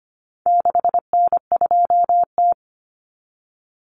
Morse code, heard as a keyed tone, about 25 wpm, 700 Hz. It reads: 6N2T
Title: Common Cut Numbers - 4-Digit 25wpm